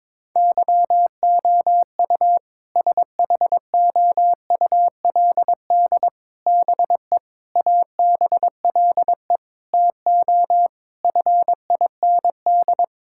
Morse code, heard as a keyed tone, about 22 wpm, 700 Hz. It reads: YOU SHOULD BE ABLE TO FIND
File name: practice / 22wpm / Sentences with 4-Word Phrases - 22wpm